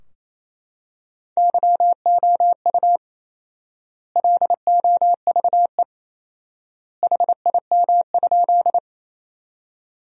Morse code, heard as a keyed tone, about 27 wpm, 700 Hz. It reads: YOU LOVE HIM?
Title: 4x Word-Spacing - Sentences from Top 400 Words - 28wpm